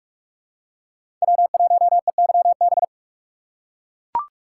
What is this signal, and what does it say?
Morse code, heard as a keyed tone, about 41 wpm, 700 Hz. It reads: W1EYB
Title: USA Call Signs - 6x Character Spacing - 45wpm